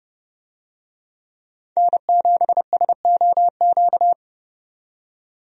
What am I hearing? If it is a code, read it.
Morse code, heard as a keyed tone, about 30 wpm, 700 Hz. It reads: N7SOQ
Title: USA Call Signs - 5 Characters - 30wpm